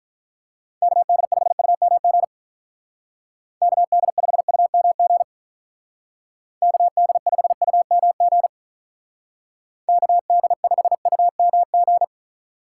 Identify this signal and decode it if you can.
Morse code, heard as a keyed tone, about 43 wpm, 700 Hz. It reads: KD5UMG KD5UMG KD5UMG KD5UMG
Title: Speed Racing - USA Call Signs 35wpm